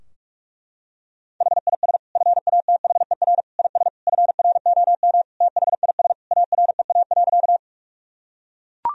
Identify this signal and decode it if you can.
Morse code, heard as a keyed tone, about 41 wpm, 700 Hz. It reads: HIS FATHER IS FROM THIS AREA.
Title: Sentences from Top 1000 Words - Review Part 2 - 45wpm